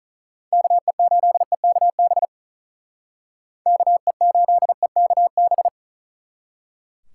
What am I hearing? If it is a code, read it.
Morse code, heard as a keyed tone, about 44 wpm, 700 Hz. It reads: KE8EKB KE8EKB